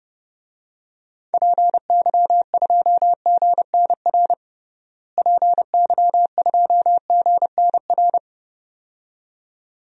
Morse code, heard as a keyed tone, about 30 wpm, 700 Hz. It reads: PY2GNR PY2GNR
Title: DXCC-100 Call Signs - 6 Characters - Repeated 2x - 30wpm